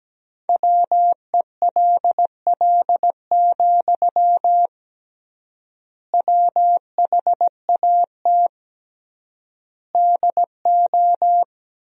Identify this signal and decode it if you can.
Morse code, heard as a keyed tone, about 17 wpm, 700 Hz. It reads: WELL, WHAT DO
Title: 3x Word-Spacing - Sentences from Top 200 Words - 17wpm